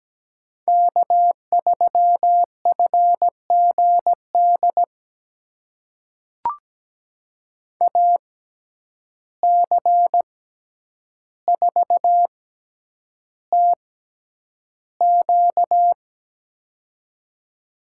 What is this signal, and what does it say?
Morse code, heard as a keyed tone, about 17 wpm, spaced as for 9 wpm, 700 Hz. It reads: K3FGD AC4TQ
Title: USA Call Signs - 5 Characters - 6x Character Spacing - 17wpm